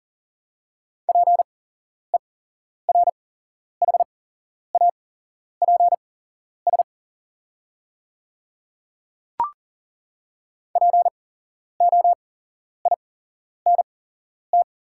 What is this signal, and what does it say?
Morse code, heard as a keyed tone, about 37 wpm, spaced as for 11 wpm, 690 Hz. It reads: PERHAPS POINT
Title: 8x Character Spacing - Top 500 Words - 40wpm